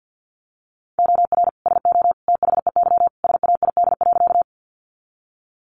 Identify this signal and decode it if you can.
Morse code, heard as a keyed tone, about 50 wpm, 700 Hz. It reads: OR SO THEY SAID.
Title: Sentences from Top 100 Words 50wpm